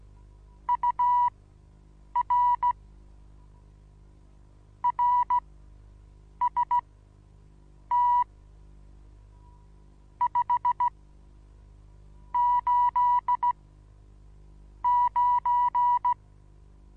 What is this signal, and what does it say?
Morse code, heard as a keyed tone, about 16 wpm, spaced as for 6 wpm, 980 Hz. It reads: UR RST 589
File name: Onlinelehrgang / morsen / DJ4UF Morsen25